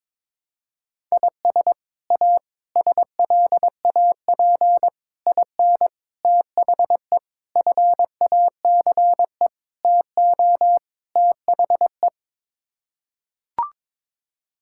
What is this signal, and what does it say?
Morse code, heard as a keyed tone, about 22 wpm, 700 Hz. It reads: IS A SLAP IN THE FACE TO THE
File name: practice / 22wpm / Sets of 8 Words - 22wpm